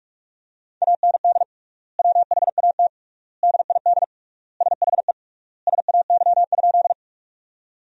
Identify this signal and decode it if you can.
Morse code, heard as a keyed tone, about 42 wpm, 700 Hz. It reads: AND WHAT DID SHE SAY?
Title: 3x Word-Spacing - Sentences from Top 200 Words - 45wpm